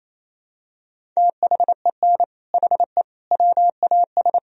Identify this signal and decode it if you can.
Morse code, heard as a keyed tone, about 28 wpm, 700 Hz. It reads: THEN HE WAS
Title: Sentences from Top 1000 Words - Review Part 2 - 28wpm